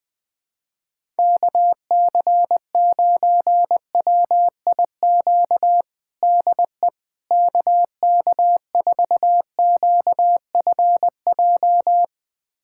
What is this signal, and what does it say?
Morse code, heard as a keyed tone, about 21 wpm, 700 Hz. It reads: KC9WIQ DE KK4QFJ
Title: USA Callsign DE Callsign - 6 Characters - 20wpm